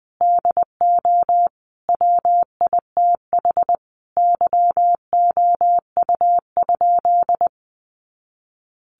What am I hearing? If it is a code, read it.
Morse code, heard as a keyed tone, about 20 wpm, 700 Hz. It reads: DO WITH YOU?
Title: Sentences from Top 100 Words 20wpm